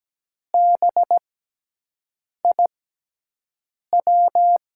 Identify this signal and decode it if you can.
Morse code, heard as a keyed tone, about 17 wpm, spaced as for 6 wpm, 700 Hz. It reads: BIW
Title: USA Call Signs - 6 Characters - 6x Character Spacing - 17wpm